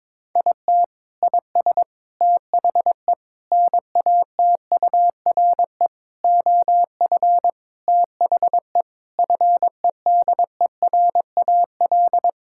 Morse code, heard as a keyed tone, about 22 wpm, 700 Hz. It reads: IT IS THE NATURE OF THE FEDERAL
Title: Sentences from Top 1000 Words - Review Part 5 - 22wpm